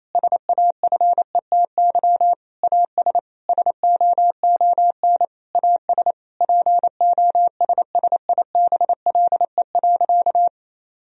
Morse code, heard as a keyed tone, about 28 wpm, 700 Hz. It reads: SAFETY AS SOON AS POSSIBLE.
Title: Sentences from Top 1000 Words - Review Part 4 - 28wpm